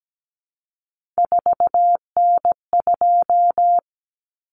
Morse code, heard as a keyed tone, about 17 wpm, 700 Hz. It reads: 4N2
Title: Common Cut Numbers - 3-Digit 17wpm